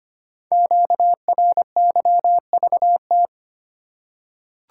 Morse code, heard as a keyed tone, about 25 wpm, 690 Hz. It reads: QRYVT
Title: Code Groups - 5 Letter:Numbers - 25wpm